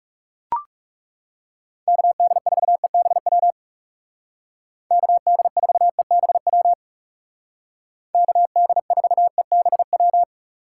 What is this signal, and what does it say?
Morse code, heard as a keyed tone, about 35 wpm, 700 Hz. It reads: KD4EBW KD4EBW KD4EBW
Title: Speed Racing - USA 2x3 Call Signs 30wpm